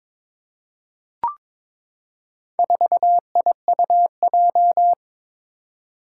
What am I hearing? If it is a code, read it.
Morse code, heard as a keyed tone, about 22 wpm, 700 Hz. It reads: 4IUJ
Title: Four Challenging Letter-Number Combinations 22wpm